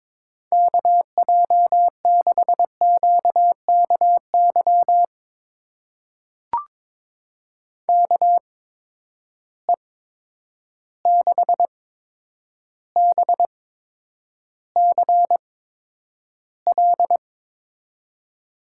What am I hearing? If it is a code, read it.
Morse code, heard as a keyed tone, about 21 wpm, spaced as for 9 wpm, 700 Hz. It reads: KJ6QKY KE6BCL
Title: USA Call Signs - 6 Characters - 8x Character Spacing - 22wpm